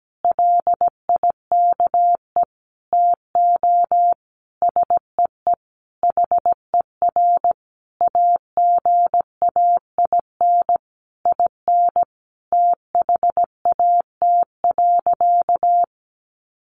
Morse code, heard as a keyed tone, about 17 wpm, 700 Hz. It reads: LIKE TO SEE HER AGAIN IN THAT.
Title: Sentences from Top 200 Words 17wpm